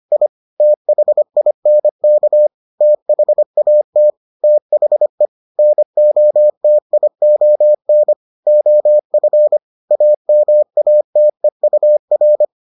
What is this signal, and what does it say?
Morse code, heard as a keyed tone, about 25 wpm, 600 Hz. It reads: I THINK THAT THE NOTION OF AMATEUR